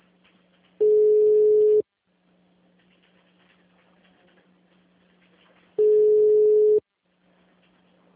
Ringback tone (Europe, 425 Hz).
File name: out- -2011- - - .29604